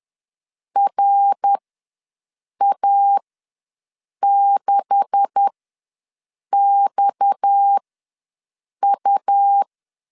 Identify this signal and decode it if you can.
Morse code, heard as a keyed tone, about 10 wpm, spaced as for 6 wpm, 800 Hz. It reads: RA6XU
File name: LEZIONI / 10 SIGNS / 10C DOT ws 10wpm 4213